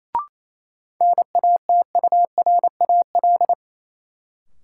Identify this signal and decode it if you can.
Morse code, heard as a keyed tone, about 28 wpm, 700 Hz. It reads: NATURAL
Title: Sentences from Top 700 Words - New Word Drill - 28wpm